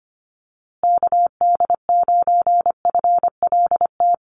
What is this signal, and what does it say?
Morse code, heard as a keyed tone, about 25 wpm, 700 Hz. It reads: KD9FLT
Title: USA Call Signs Repeated 3x 25wpm